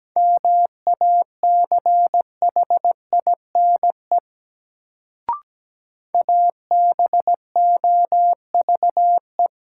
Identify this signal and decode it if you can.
Morse code, heard as a keyed tone, about 17 wpm, 700 Hz. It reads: MACHINE ABOVE
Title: Sentences from Top 1000 Words - Review Part 2 - New Word Drill - 17wpm